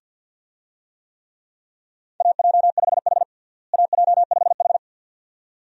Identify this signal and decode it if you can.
Morse code, heard as a keyed tone, about 52 wpm, 700 Hz. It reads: AJ5H AJ5H
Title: USA Call Signs - 4 Characters - Repeated 2x - 50wpm